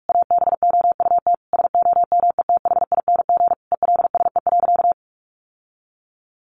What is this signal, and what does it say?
Morse code, heard as a keyed tone, about 44 wpm, 700 Hz. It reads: ABOUT SOMETHING ELSE.
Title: Sentences from Top 800 Words 45wpm